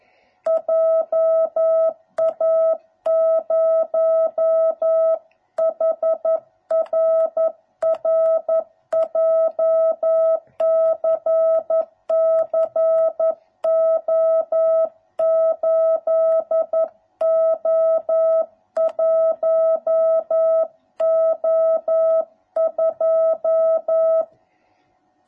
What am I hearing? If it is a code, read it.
Morse code, heard as a keyed tone, about 11 wpm, 650 Hz. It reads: JA0HRR JCC O8O1O2